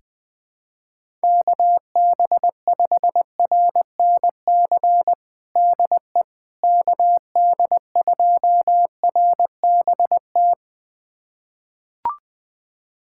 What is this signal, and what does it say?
Morse code, heard as a keyed tone, about 21 wpm, 700 Hz. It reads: KB5RNC DE KD2RBT
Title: USA Callsign DE Callsign - 6 Characters - 20wpm